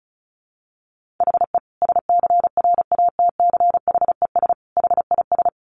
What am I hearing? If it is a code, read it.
Morse code, heard as a keyed tone, about 34 wpm, 700 Hz. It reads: HE SCRATCHES HIS
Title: Easy Phrases 35wpm